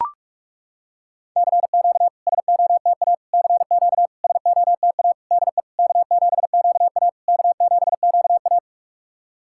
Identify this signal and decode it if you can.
Morse code, heard as a keyed tone, about 42 wpm, 690 Hz. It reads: CQ SOTA CQ SOTA DE K7QA K7QA
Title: SOTA QSO Practice 45wpm